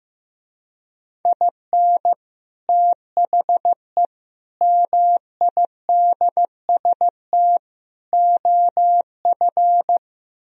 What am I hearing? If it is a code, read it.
Morse code, heard as a keyed tone, about 15 wpm, 700 Hz. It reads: IN THE MIDST OF
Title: Speed-Racing - Sets of 4 Words 15wpm